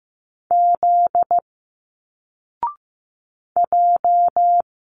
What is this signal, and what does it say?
Morse code, heard as a keyed tone, about 15 wpm, 700 Hz. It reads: Z J
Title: Single Letters 15wpm